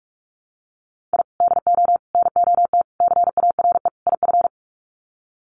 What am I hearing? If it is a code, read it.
Morse code, heard as a keyed tone, about 44 wpm, 700 Hz. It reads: I DO NOT CARE IF
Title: Sets of 5 Words 45wpm